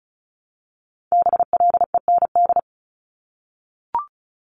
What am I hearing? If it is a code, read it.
Morse code, heard as a keyed tone, about 36 wpm, 700 Hz. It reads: BLEND
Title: Multi-letter Phonemes Repeated 3x 35wpm